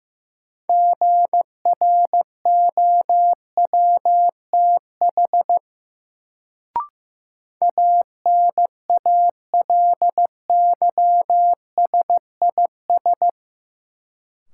Morse code, heard as a keyed tone, about 15 wpm, 700 Hz. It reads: GROWTH ANALYSIS
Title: Sentences from Top 1000 Words - Review Part 5 - New Word Drill - 15wpm